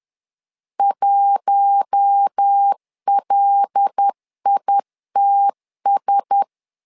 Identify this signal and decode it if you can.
Morse code, heard as a keyed tone, about 10 wpm, 800 Hz. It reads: 1LITS
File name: LEZIONI / 5 LBVC / 5 L 10wpm 2259